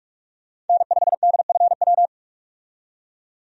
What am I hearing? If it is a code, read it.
Morse code, heard as a keyed tone, about 46 wpm, 700 Hz. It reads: N5DFW
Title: USA Call Signs - 5 Characters - 45wpm